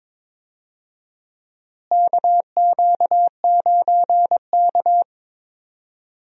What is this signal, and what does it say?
Morse code, heard as a keyed tone, about 22 wpm, 700 Hz. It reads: KQ9K
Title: USA Call Signs - 4 Characters - Repeated 2x - 22wpm